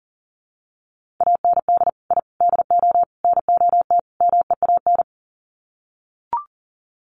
Morse code, heard as a keyed tone, about 40 wpm, 700 Hz. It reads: AND I DO NOT MEAN
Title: Sets of 5 Words Encore2 40wpm